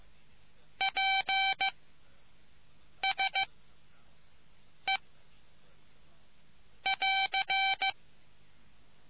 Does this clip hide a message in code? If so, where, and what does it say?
Yes: Morse code from 0.8 to 7.9 s, heard as a keyed tone, about 15 wpm, spaced as for 5 wpm, 2300 Hz. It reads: PSE <AR>